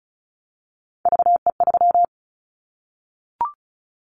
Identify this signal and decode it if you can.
Morse code, heard as a keyed tone, about 34 wpm, 700 Hz. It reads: VE3
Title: Callsign Prefixes for Top 100 DXCC - 35wpm